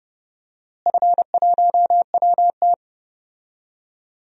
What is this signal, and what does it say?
Morse code, heard as a keyed tone, about 30 wpm, 700 Hz. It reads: F1WT
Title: DXCC-100 Call Signs - 4 Characters - Repeated 3x - 30wpm